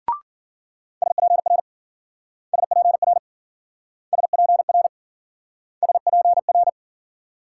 Morse code, heard as a keyed tone, about 52 wpm, 700 Hz. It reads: SPR SPR SPR SPR